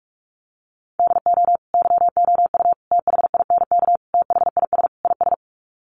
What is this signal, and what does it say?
Morse code, heard as a keyed tone, about 44 wpm, 700 Hz. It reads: DO YOU THINK THIS IS
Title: Sets of 5 Words Encore2 45wpm